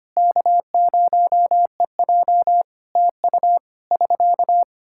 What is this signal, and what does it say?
Morse code, heard as a keyed tone, about 25 wpm, 700 Hz. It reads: K0EJ TU <SK>
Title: Practice QSO - Exchange 4 - Ending - 25wpm